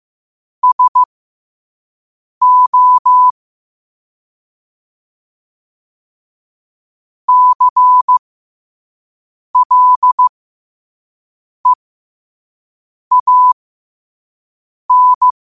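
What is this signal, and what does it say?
Morse code, heard as a keyed tone, about 15 wpm, spaced as for 5 wpm, 1000 Hz. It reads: SO CLEAN
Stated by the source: Morse code practice at 5 wpm.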